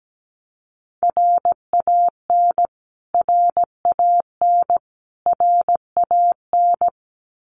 Morse code, heard as a keyed tone, about 17 wpm, 700 Hz. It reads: RAN RAN RAN
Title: Top 500 Words Repeated 17wpm